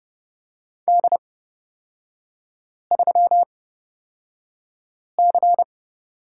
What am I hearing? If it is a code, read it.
Morse code, heard as a keyed tone, about 30 wpm, spaced as for 5 wpm, 700 Hz. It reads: D3C
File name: Lesson 38e - Letter q - Three Characters 30wpm